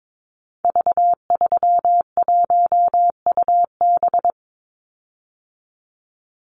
Morse code, heard as a keyed tone, about 22 wpm, 700 Hz. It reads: V31UB
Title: Call Signs Repeated 2x - 22wpm